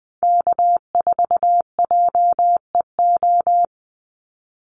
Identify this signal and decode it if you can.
Morse code, heard as a keyed tone, about 20 wpm, 700 Hz. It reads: K4JEO
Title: USA Call Signs 20wpm